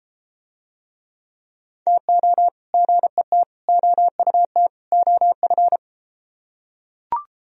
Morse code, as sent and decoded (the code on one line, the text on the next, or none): - --- / --. - / --- ..- - / --- ..-.
TO GET OUT OF